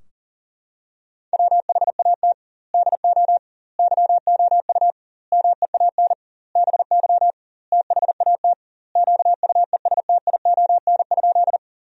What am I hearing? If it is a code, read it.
Morse code, heard as a keyed tone, about 40 wpm, 700 Hz. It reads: WHAT DO YOU MEAN BY THAT QUESTION?
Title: 2x Word-Spacing - Sentences from Top 400 Words - 40wpm